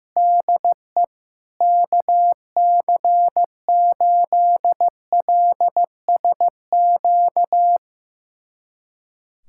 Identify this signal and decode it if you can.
Morse code, heard as a keyed tone, about 15 wpm, 700 Hz. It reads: DE KC8LSQ
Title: USA Callsign DE Callsign - 6 Characters - 15wpm